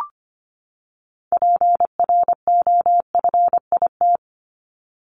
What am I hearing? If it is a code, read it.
Morse code, heard as a keyed tone, about 25 wpm, 700 Hz. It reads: PROFIT
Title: Top 751-1000 Words 25wpm